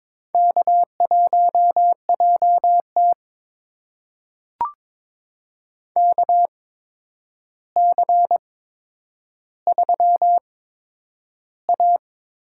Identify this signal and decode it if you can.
Morse code, heard as a keyed tone, about 22 wpm, spaced as for 9 wpm, 700 Hz. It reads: K1JT KC3A
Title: USA Call Signs - 4 Characters - 8x Character Spacing - 22wpm